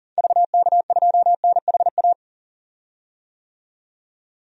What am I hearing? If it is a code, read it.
Morse code, heard as a keyed tone, about 40 wpm, 700 Hz. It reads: VK2NHA
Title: DXCC-100 Call Signs - 6 Characters - 40wpm